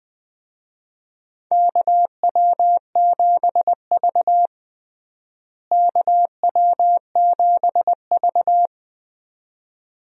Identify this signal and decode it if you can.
Morse code, heard as a keyed tone, about 20 wpm, 700 Hz. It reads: KW7V KW7V
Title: USA Call Signs - 4 Characters - Repeated 2x - 20wpm